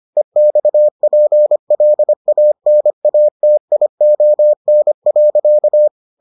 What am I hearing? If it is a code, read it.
Morse code, heard as a keyed tone, about 25 wpm, 600 Hz. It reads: EXPLANATION.